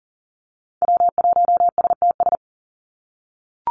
Morse code, heard as a keyed tone, about 40 wpm, 700 Hz. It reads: W1STS
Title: USA Call Signs 40wpm